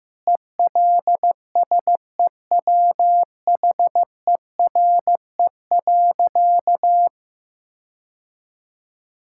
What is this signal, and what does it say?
Morse code, heard as a keyed tone, about 15 wpm, 700 Hz. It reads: ELSEWHERE.
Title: Sentences from Top 1600 Words 15wpm